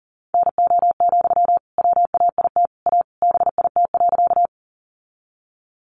Morse code, heard as a keyed tone, about 39 wpm, 700 Hz. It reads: NO, WAIT A BIT.
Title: Sentences from Top 900 Words 40wpm